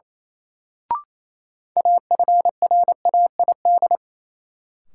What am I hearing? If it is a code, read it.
Morse code, heard as a keyed tone, about 28 wpm, 700 Hz. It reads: AFRAID
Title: Sentences from Top 1500 Words - New Word Drill - 28wpm